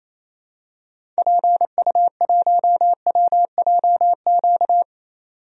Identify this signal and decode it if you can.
Morse code, heard as a keyed tone, about 28 wpm, 700 Hz. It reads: PU1WJQ
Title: DXCC-100 Call Signs - 6 Characters - 28wpm